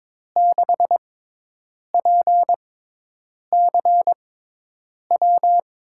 Morse code, heard as a keyed tone, about 23 wpm, spaced as for 8 wpm, 700 Hz. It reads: 6PCW
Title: USA Call Signs - 6x Character Spacing - 22wpm